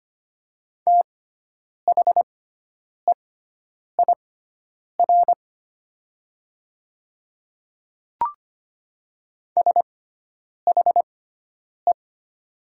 Morse code, heard as a keyed tone, about 25 wpm, spaced as for 9 wpm, 700 Hz. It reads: THEIR SHE
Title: 6x Character Spacing - Top 100 Words - 25wpm